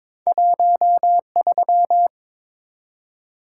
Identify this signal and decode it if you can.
Morse code, heard as a keyed tone, about 22 wpm, 700 Hz. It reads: 13
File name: practice / 22wpm / Two Letter-Number Combinations - 22wpm